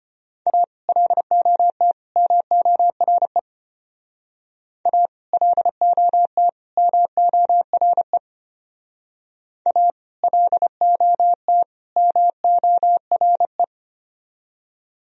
Morse code, heard as a keyed tone, about 29 wpm, 700 Hz. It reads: A LOT MORE A LOT MORE A LOT MORE
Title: Speed-Racing - Sets of 3 Words 25wpm